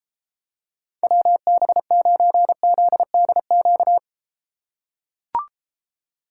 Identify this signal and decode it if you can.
Morse code, heard as a keyed tone, about 32 wpm, 700 Hz. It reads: WB9ZDQ